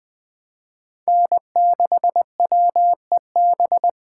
Morse code, heard as a keyed tone, about 24 wpm, 700 Hz. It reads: N6WEB
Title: Speed Racing - USA Call Signs 20wpm